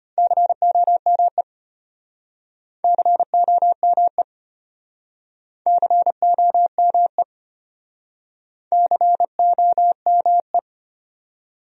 Morse code, heard as a keyed tone, about 33 wpm, 700 Hz. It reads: COME COME COME COME